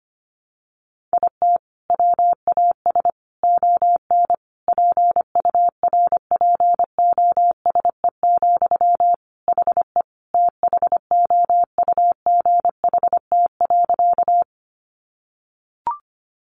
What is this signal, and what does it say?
Morse code, heard as a keyed tone, about 25 wpm, 700 Hz. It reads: IT WAS ON PURPOSE, HE THOUGHT.
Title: Sentences from Top 800 Words 25wpm